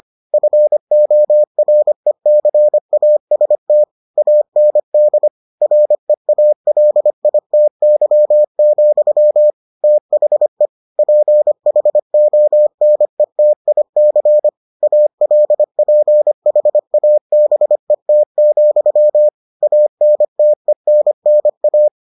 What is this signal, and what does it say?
Morse code, heard as a keyed tone, about 25 wpm, 600 Hz. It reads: FORECAST AND REALITY, THE PHONETIC ALPHABET, ANTENNA